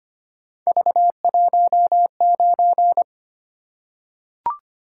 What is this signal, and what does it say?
Morse code, heard as a keyed tone, about 25 wpm, 700 Hz. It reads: V19
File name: Three Letter-Number Combinations - 25wpm